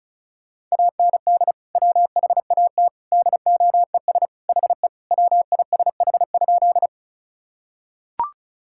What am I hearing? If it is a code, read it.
Morse code, heard as a keyed tone, about 34 wpm, 700 Hz. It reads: AND WHAT DOES HE WISH?
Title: Sentences from Top 1600 Words 35wpm